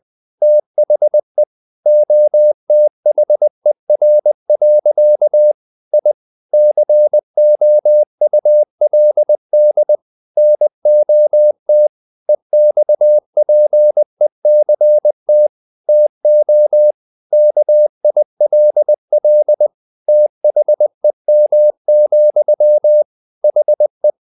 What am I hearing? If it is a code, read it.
Morse code, heard as a keyed tone, about 20 wpm, 600 Hz. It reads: THE OTHER. I COULD NOT EXPECT TO KILL THEM, HE
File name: the old man and the sea part 2 20wpm